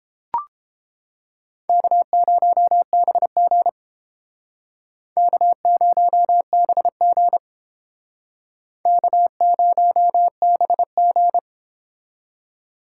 Morse code, heard as a keyed tone, about 32 wpm, 700 Hz. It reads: K0BG K0BG K0BG